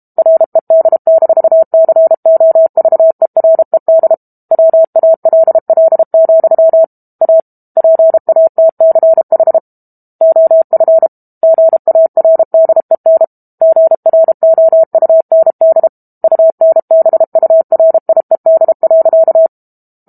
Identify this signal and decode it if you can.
Morse code, heard as a keyed tone, about 32 wpm, 650 Hz. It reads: RED-COVERED WALL, A PATCH OF GARDEN GROUND UNBURIED.
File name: Book Two / 06-Chapter 6 - 32 WPM